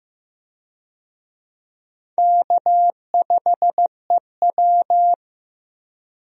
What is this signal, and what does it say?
Morse code, heard as a keyed tone, about 16 wpm, 700 Hz. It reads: K5EW
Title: USA Call Signs - 4 Characters - Repeated 3x - 15wpm